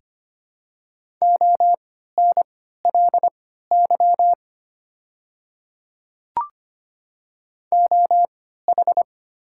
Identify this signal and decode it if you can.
Morse code, heard as a keyed tone, about 25 wpm, spaced as for 14 wpm, 690 Hz. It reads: ONLY OH
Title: 3x Character Spacing - Top 500 Words - 25wpm